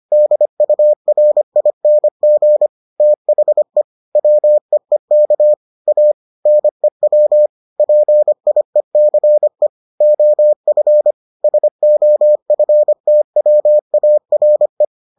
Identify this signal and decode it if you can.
Morse code, heard as a keyed tone, about 25 wpm, 600 Hz. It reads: DURING THE WEEK A NEW PIECE OF SOFTWARE